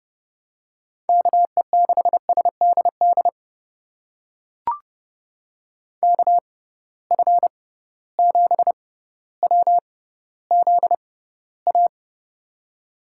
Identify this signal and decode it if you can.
Morse code, heard as a keyed tone, about 30 wpm, spaced as for 15 wpm, 700 Hz. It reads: KE6SDD KF7WZA
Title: USA Call Signs - 6 Characters - 6x Character Spacing - 30wpm